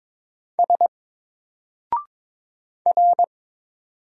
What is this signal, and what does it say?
Morse code, heard as a keyed tone, about 22 wpm, 700 Hz. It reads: S R
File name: Single-Letter-Number-22wpm